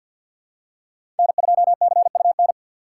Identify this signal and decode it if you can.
Morse code, heard as a keyed tone, about 47 wpm, 700 Hz. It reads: N2CUN
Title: USA Call Signs - 5 Characters - 3x Character Spacing - 50wpm